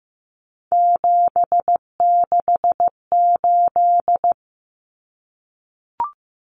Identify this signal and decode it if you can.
Morse code, heard as a keyed tone, about 15 wpm, 700 Hz. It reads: 768
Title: Common Cut Numbers - 3-Digit 15wpm